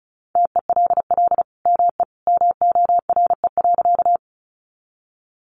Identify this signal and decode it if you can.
Morse code, heard as a keyed tone, about 34 wpm, 700 Hz. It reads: TELL ME MORE.
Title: Sentences from Top 300 Words 35wpm